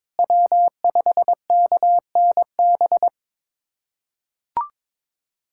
Morse code, heard as a keyed tone, about 22 wpm, 700 Hz. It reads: W5KNB
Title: USA Callsign DE Callsign - 5 Characters - 22wpm